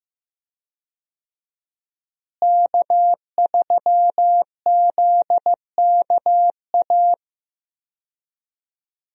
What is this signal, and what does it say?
Morse code, heard as a keyed tone, about 15 wpm, 700 Hz. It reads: K3ZKA
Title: USA Call Signs - 5 Characters - Repeated 2x - 15wpm